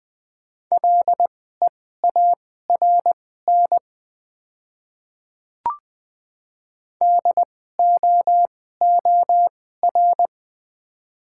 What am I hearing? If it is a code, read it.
Morse code, heard as a keyed tone, about 20 wpm, spaced as for 14 wpm, 700 Hz. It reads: LEARN DOOR
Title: 2x Character Spacing - Top 300 Words - 20wpm